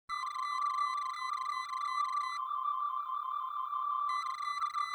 There is no code.